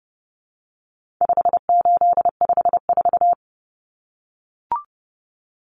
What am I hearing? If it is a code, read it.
Morse code, heard as a keyed tone, about 30 wpm, 700 Hz. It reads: 5854
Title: Common Cut Numbers - 4-Digit 30wpm